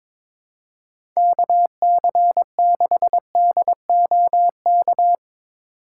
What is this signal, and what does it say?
Morse code, heard as a keyed tone, about 22 wpm, 700 Hz. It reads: KC6DOK
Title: USA Call Signs - 6 Characters - 2x Character Spacing - 22wpm